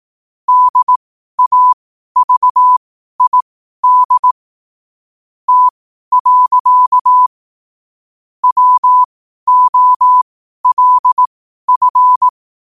Morse code, heard as a keyed tone, about 18 wpm, spaced as for 12 wpm, 1000 Hz. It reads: DAVID T. WOLF
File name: qotdfiles / 12wpm